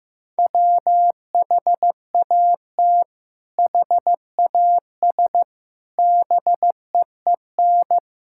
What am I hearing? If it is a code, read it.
Morse code, heard as a keyed tone, about 15 wpm, 700 Hz. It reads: WHAT HAS BEEN
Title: Sentences from Top 1000 Words - Review Part 6 - 15wpm